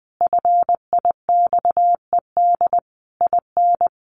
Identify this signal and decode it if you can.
Morse code, heard as a keyed tone, about 20 wpm, 700 Hz. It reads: FIXED IN
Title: Sentences from Top 1100 Words 20wpm